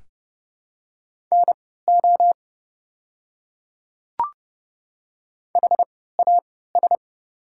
Morse code, heard as a keyed tone, about 32 wpm, spaced as for 17 wpm, 700 Hz. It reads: NO HAS
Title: 3x Character Spacing - Top 100 Words - 30wpm